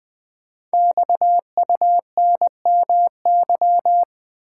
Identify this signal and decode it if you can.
Morse code, heard as a keyed tone, about 20 wpm, 700 Hz. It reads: XUNMY
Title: Code Groups - 5 Letters - 20wpm